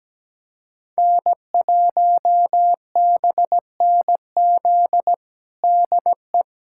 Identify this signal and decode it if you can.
Morse code, heard as a keyed tone, about 17 wpm, 700 Hz. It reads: N1BNZ DE
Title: USA Callsign DE Callsign - 5 Characters - 17wpm